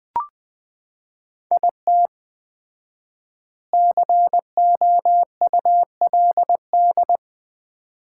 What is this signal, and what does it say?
Morse code, heard as a keyed tone, about 20 wpm, 700 Hz. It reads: IT COULD
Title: 4x Word-Spacing - Sets of 3 Words - 20wpm